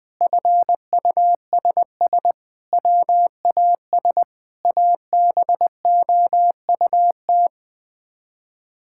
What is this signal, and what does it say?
Morse code, heard as a keyed tone, about 20 wpm, 700 Hz. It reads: FUSS WAS ABOUT
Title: Sets of 8 Words - 20wpm